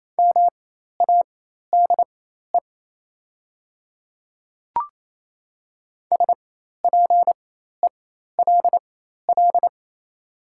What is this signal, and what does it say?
Morse code, heard as a keyed tone, about 28 wpm, spaced as for 13 wpm, 700 Hz. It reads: MADE SPELL
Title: 4x Character Spacing - Top 200 Words - 28wpm